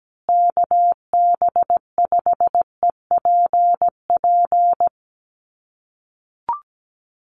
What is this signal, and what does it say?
Morse code, heard as a keyed tone, about 17 wpm, 700 Hz. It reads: KB5EPP
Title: USA Call Signs Repeated 2x 17wpm